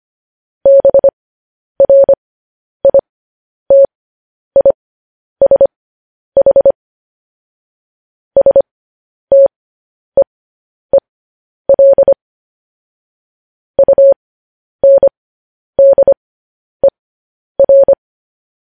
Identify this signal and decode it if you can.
Morse code, heard as a keyed tone, about 25 wpm, spaced as for 10 wpm, 550 Hz. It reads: BRITISH STEEL UNDER
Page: news headlines in morse code at 10 wpm